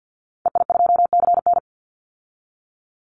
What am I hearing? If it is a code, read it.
Morse code, heard as a keyed tone, about 48 wpm, 700 Hz. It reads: EI2CN
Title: Call Signs Repeated 3x - 50wpm